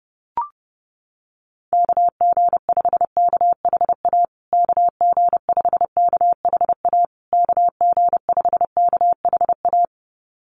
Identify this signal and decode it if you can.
Morse code, heard as a keyed tone, about 30 wpm, 700 Hz. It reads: KG5KHA KG5KHA KG5KHA
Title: USA Call Signs Repeated 3x 30wpm